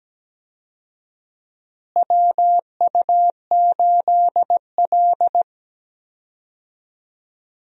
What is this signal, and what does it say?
Morse code, heard as a keyed tone, about 17 wpm, 700 Hz. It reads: WU8L
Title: USA Call Signs - 4 Characters - Repeated 3x - 17wpm